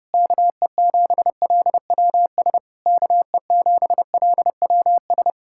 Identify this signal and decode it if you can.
Morse code, heard as a keyed tone, about 33 wpm, 700 Hz. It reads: KE7LWS KE7LWS